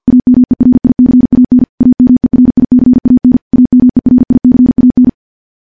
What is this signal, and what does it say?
Morse code, heard as a keyed tone, about 51 wpm, 250 Hz. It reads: AMERICAN AMERICAN AMERICAN
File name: morse / morse-training-50WPM